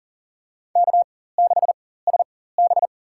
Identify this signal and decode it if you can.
Morse code, heard as a keyed tone, about 40 wpm, spaced as for 19 wpm, 700 Hz. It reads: K6SB
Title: USA Call Signs - 4 Characters - 4x Character Spacing - 40wpm